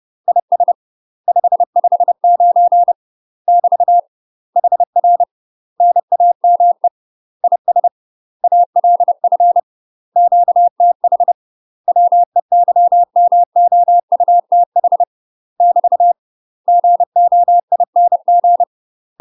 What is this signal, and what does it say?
Morse code, heard as a keyed tone, about 30 wpm, 700 Hz. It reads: IS 559 <BT> HR NAME IS ALF QTH WEYMOUTH <BT> GOING
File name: UK MASTER QSOs TRAINING / 30WPM / UK MASTER QSOs 0005